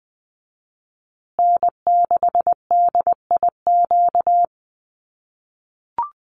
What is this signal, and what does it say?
Morse code, heard as a keyed tone, about 20 wpm, 700 Hz. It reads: N6DIQ
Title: USA Call Signs Repeated 3x 20wpm